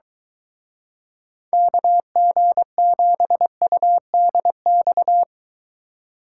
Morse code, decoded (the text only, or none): KG7UDX